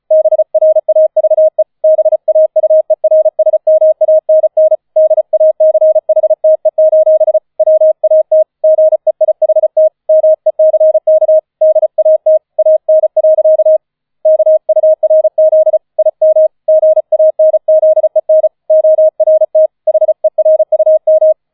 Morse code, heard as a keyed tone, about 34 wpm, 630 Hz. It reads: BRAVE BAUERSMANN DACHTE: WAT GEIHT MECK DAT AN. KURZ IM GANZEN ORT HERUM